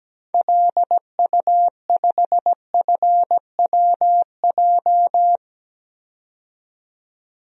Morse code, heard as a keyed tone, about 17 wpm, 700 Hz. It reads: LU5FWJ
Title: DXCC-100 Call Signs - 6 Characters - 17wpm